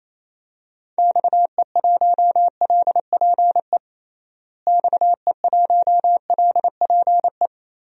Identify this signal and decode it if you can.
Morse code, heard as a keyed tone, about 28 wpm, 700 Hz. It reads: XE1LPE XE1LPE
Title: DXCC-100 Call Signs - 6 Characters - Repeated 3x - 28wpm